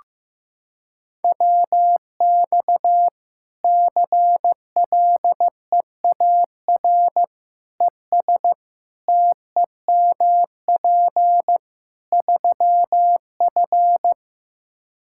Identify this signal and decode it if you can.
Morse code, heard as a keyed tone, about 15 wpm, 700 Hz. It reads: WX CLEAR ES TEMP 3F
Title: Practice QSO - Exchange 2 - Info - 15wpm